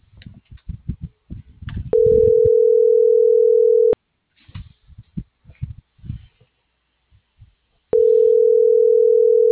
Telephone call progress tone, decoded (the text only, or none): ringback tone